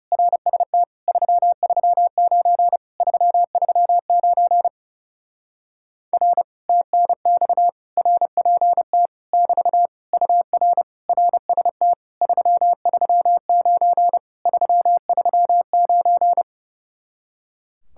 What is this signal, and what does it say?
Morse code, heard as a keyed tone, about 30 wpm, 700 Hz. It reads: RST 339 339 R TNX RPT <BT> UR RST 339 339
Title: Speed-Racing Common QSO Phrases Encore - 30wpm